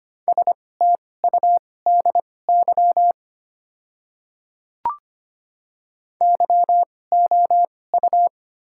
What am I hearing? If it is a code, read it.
Morse code, heard as a keyed tone, about 25 wpm, spaced as for 18 wpm, 700 Hz. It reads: STUDY YOU
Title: 2x Character Spacing - Top 300 Words - 25wpm